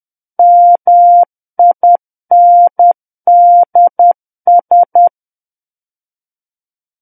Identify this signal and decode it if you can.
Morse code, heard as a keyed tone, about 10 wpm, 700 Hz. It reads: MINDS
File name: Book One / 01-Chapter 1 - 10 WPM